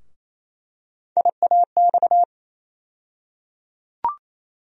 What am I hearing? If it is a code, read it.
Morse code, heard as a keyed tone, about 28 wpm, 700 Hz. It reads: IAX
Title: USA Call Sign Postfixes - 3 Letters 28wpm